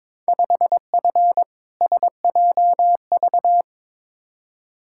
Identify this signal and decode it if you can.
Morse code, heard as a keyed tone, about 22 wpm, 700 Hz. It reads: 5F SJV
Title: Field Day Exchanges Encore - 22wpm